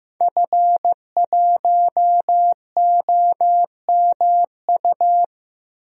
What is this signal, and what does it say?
Morse code, heard as a keyed tone, about 15 wpm, 700 Hz. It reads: F1OMU
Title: DXCC-100 Call Signs - 5 Characters - 15wpm